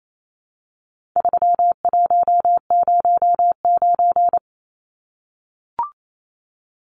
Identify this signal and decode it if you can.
Morse code, heard as a keyed tone, about 28 wpm, 700 Hz. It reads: 3109
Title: Numbers - 4-Digit 28wpm